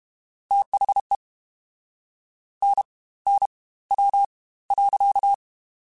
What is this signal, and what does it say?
Morse code, heard as a keyed tone, about 32 wpm, spaced as for 19 wpm, 800 Hz. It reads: THE NNW.
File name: Spaced Characters / Spaced Characters-mp3s / Weather5 22 sp @30